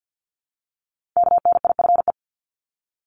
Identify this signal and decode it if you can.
Morse code, heard as a keyed tone, about 50 wpm, 700 Hz. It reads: KNIFE
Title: Multi-letter Phonemes Repeated 3x 50wpm